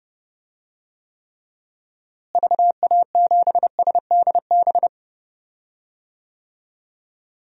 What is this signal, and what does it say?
Morse code, heard as a keyed tone, about 30 wpm, 700 Hz. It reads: VA7SDB
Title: DXCC-100 Call Signs - 6 Characters - Repeated 2x - 30wpm